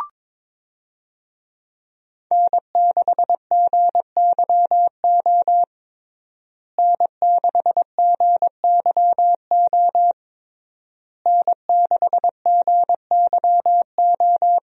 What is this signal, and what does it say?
Morse code, heard as a keyed tone, about 22 wpm, 700 Hz. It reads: N6GYO N6GYO N6GYO
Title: USA Call Signs - 5 Characters - Repeated 3x - 22wpm